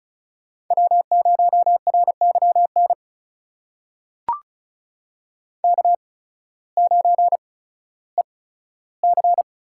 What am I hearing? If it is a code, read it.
Morse code, heard as a keyed tone, about 36 wpm, spaced as for 16 wpm, 700 Hz. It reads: W0RYN K9EC
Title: USA Call Signs - 8x Character Spacing - 35wpm